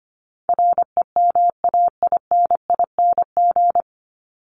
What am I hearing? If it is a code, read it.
Morse code, heard as a keyed tone, about 25 wpm, 700 Hz. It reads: REMAINING
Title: ING Suffix - Top 100 Words with Base Words 25wpm